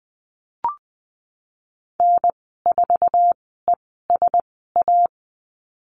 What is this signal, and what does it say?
Morse code, heard as a keyed tone, about 20 wpm, spaced as for 15 wpm, 700 Hz. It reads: N4ESA
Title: 055 Lesson 10g - Letter r - Call Signs 20wpm